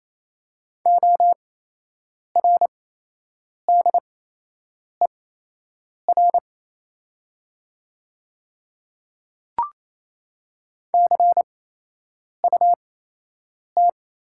Morse code, heard as a keyed tone, about 28 wpm, spaced as for 8 wpm, 700 Hz. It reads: ORDER CUT
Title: 8x Character Spacing - Top 500 Words - 28wpm